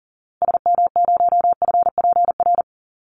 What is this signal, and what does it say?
Morse code, heard as a keyed tone, about 40 wpm, 700 Hz. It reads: SM0FPR
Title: Call Signs Repeated 3x - 40wpm